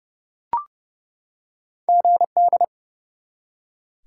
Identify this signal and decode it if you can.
Morse code, heard as a keyed tone, about 30 wpm, 700 Hz. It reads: GD
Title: Top 90 QSO Elements 30wpm